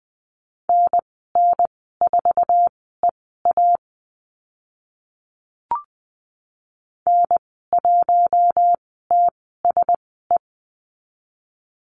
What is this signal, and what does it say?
Morse code, heard as a keyed tone, about 20 wpm, spaced as for 14 wpm, 700 Hz. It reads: NN4EA N1TSE
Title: 055 Lesson 10g - Letter r - Call Signs 20wpm